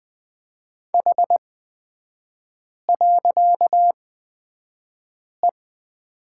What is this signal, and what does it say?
Morse code, heard as a keyed tone, about 20 wpm, spaced as for 5 wpm, 700 Hz. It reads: H.E
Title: 255 Lesson 41e - Period - Three Characters 20wpm